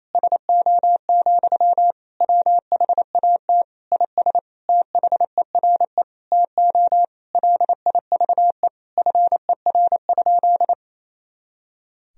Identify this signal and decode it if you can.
Morse code, heard as a keyed tone, about 28 wpm, 700 Hz. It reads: SO, WHAT IS THERE TO LIVE FER?
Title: Sentences from Top 200 Words and Abbreviations 28wpm